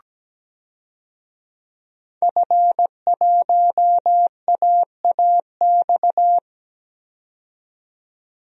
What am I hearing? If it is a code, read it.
Morse code, heard as a keyed tone, about 17 wpm, 700 Hz. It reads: F1AAX
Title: DXCC-100 Call Signs - 5 Characters - 17wpm